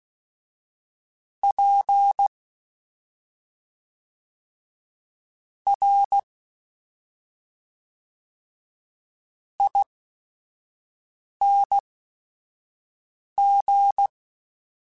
Morse code, heard as a keyed tone, about 16 wpm, spaced as for 5 wpm, 800 Hz. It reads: P R ING
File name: Special INGs / Special INGs-mp3s / SlowToFaster ing @5